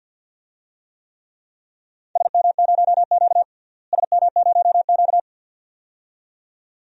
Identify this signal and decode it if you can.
Morse code, heard as a keyed tone, about 52 wpm, 700 Hz. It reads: SM0Q SM0Q
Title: DXCC-100 Call Signs - 4 Characters - Repeated 2x - 50wpm